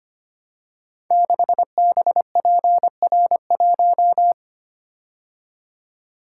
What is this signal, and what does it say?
Morse code, heard as a keyed tone, about 25 wpm, 700 Hz. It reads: 6BPR1
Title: Code Groups - 5 Letter:Numbers - 25wpm